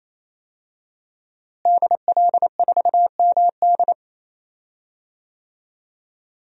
Morse code, heard as a keyed tone, about 27 wpm, 700 Hz. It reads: DL4MD
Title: DXCC-100 Call Signs - 5 Characters - Repeated 3x - 28wpm